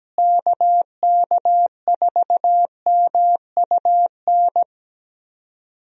Morse code, heard as a keyed tone, about 17 wpm, 700 Hz. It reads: KK4MUN
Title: USA Call Signs - 6 Characters - Repeated 2x - 17wpm